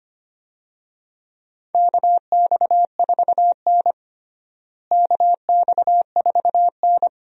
Morse code, heard as a keyed tone, about 25 wpm, 700 Hz. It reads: KX4N KX4N
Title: USA Call Signs - 4 Characters - Repeated 2x - 25wpm